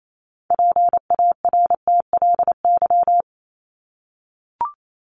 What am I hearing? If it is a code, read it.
Morse code, heard as a keyed tone, about 28 wpm, 700 Hz. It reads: PARTLY
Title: LY Suffix - Top 100 Words with Base Words 28wpm